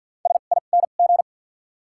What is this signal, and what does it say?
Morse code, heard as a keyed tone, about 53 wpm, spaced as for 37 wpm, 700 Hz. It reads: SING
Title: 2x Character Spacing - Top 500 Words - 50wpm